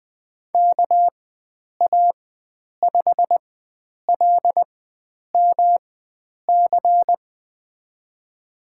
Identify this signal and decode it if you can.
Morse code, heard as a keyed tone, about 21 wpm, spaced as for 9 wpm, 700 Hz. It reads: KA5LMC
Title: USA Call Signs - 6 Characters - 4x Character Spacing - 20wpm